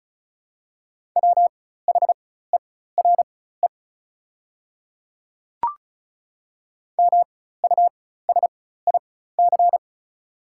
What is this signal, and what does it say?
Morse code, heard as a keyed tone, about 34 wpm, spaced as for 16 wpm, 700 Hz. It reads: WHERE MUSIC
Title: 4x Character Spacing - Top 300 Words - 35wpm